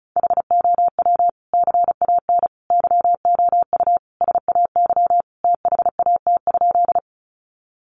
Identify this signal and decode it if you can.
Morse code, heard as a keyed tone, about 34 wpm, 700 Hz. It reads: HOW CAN YOU SAY THAT?
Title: Sentences from Top 200 Words 35wpm